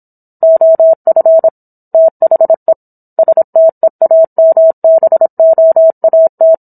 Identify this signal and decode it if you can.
Morse code, heard as a keyed tone, about 27 wpm, 650 Hz. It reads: OF THE STEAMBOAT